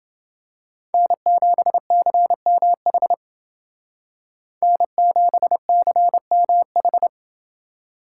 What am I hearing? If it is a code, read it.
Morse code, heard as a keyed tone, about 28 wpm, 700 Hz. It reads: N7CMH N7CMH